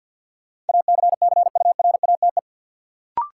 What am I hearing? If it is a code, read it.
Morse code, heard as a keyed tone, about 47 wpm, 700 Hz. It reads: ACCURATE
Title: Sentences from Top 1200 Words - New Word Drill Alt - 50wpm